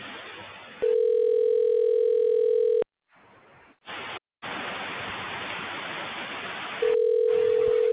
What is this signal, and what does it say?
Ringback tone (North America), 440 + 480 Hz.